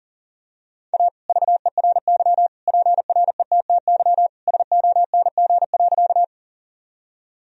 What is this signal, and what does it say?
Morse code, heard as a keyed tone, about 39 wpm, 700 Hz. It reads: A VERY PRETTY SONG.
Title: Sentences from Top 1000 Words - Review Part 2 - 40wpm